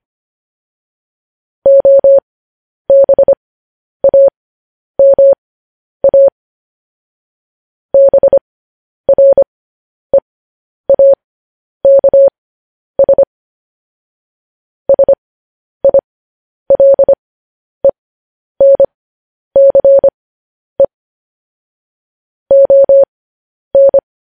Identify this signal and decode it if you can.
Morse code, heard as a keyed tone, about 25 wpm, spaced as for 10 wpm, 550 Hz. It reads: OBAMA BREAKS SILENCE ON
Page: news headlines in morse code at 10 wpm